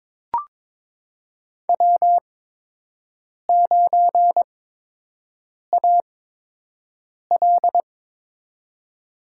Morse code, heard as a keyed tone, about 22 wpm, spaced as for 6 wpm, 700 Hz. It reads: W9AL
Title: USA Call Signs - 8x Character Spacing - 22wpm